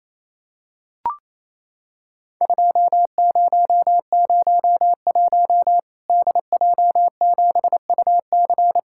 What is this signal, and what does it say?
Morse code, heard as a keyed tone, about 28 wpm, 700 Hz. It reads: 2001 DJ7UC
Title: Worked All Europe DX Contest - QTC Exchange - 28wpm